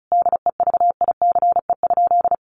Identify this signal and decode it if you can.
Morse code, heard as a keyed tone, about 34 wpm, 700 Hz. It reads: DEVICE?
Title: Sentences from Top 1500 Words 35wpm